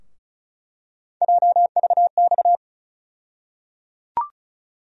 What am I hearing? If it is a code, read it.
Morse code, heard as a keyed tone, about 36 wpm, 700 Hz. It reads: JVX
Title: USA Call Sign Postfixes - 3 Letters 35wpm